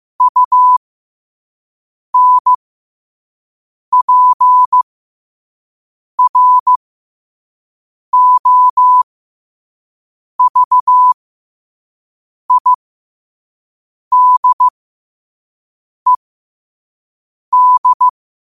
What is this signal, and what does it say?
Morse code, heard as a keyed tone, about 15 wpm, spaced as for 5 wpm, 1000 Hz. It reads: UNPROVIDED
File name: qotdfiles / 5wpm